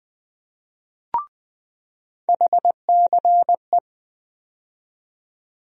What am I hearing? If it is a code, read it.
Morse code, heard as a keyed tone, about 21 wpm, 700 Hz. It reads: HCE
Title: USA Call Sign Postfixes - 3 Letters 20wpm